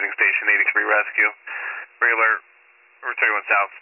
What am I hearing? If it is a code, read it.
Emergency Alert System (SAME) tones, heard as harsh buzzing bursts, with speech. It is heard from 1.0 to 3.4 s.